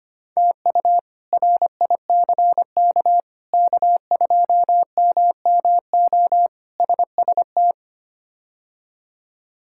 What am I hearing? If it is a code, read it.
Morse code, heard as a keyed tone, about 25 wpm, 700 Hz. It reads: TU RICK K2MMO SST
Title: K1USN SST QSO Practice - 25wpm